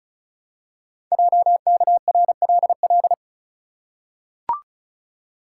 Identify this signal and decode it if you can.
Morse code, heard as a keyed tone, about 36 wpm, 700 Hz. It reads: JKRLL
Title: Code Groups - 5 Letters - 35wpm